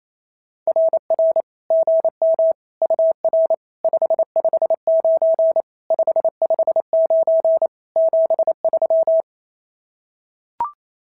Morse code, heard as a keyed tone, about 28 wpm, 650 Hz. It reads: RR GM UR 559 559 73
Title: SOTA QSO Practice 28wpm